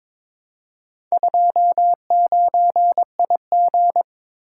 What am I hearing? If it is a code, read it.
Morse code, heard as a keyed tone, about 22 wpm, 700 Hz. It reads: 29IG
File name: practice / 22wpm / Four Letter-Number Combinations - 22wpm